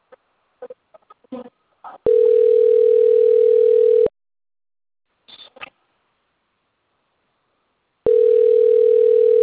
Ringback tone (North America), 440 + 480 Hz.